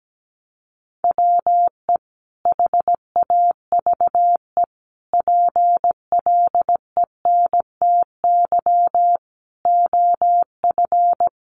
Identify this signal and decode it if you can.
Morse code, heard as a keyed tone, about 17 wpm, 700 Hz. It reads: WE HAVE PLENTY OF
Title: Sentences from Top 1300 Words 17wpm